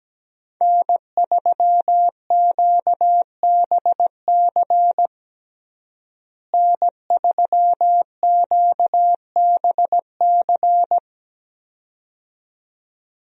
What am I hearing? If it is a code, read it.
Morse code, heard as a keyed tone, about 17 wpm, 700 Hz. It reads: N3QBC N3QBC
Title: USA Call Signs - 5 Characters - Repeated 2x - 17wpm